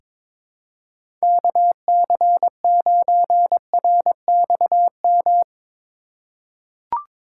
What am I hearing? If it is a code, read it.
Morse code, heard as a keyed tone, about 22 wpm, 700 Hz. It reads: KC9RXM
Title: USA Call Signs - 6 Characters - 2x Character Spacing - 22wpm